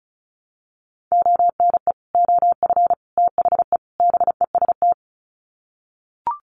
Morse code, heard as a keyed tone, about 34 wpm, 700 Hz. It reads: ONE OF THE BEST
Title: Sets of 4 Words 35wpm